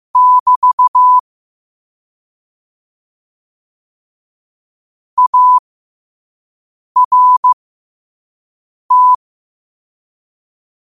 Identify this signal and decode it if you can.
Morse code, heard as a keyed tone, about 15 wpm, spaced as for 5 wpm, 1000 Hz. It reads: <BT> ART
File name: qotdfiles / 5wpm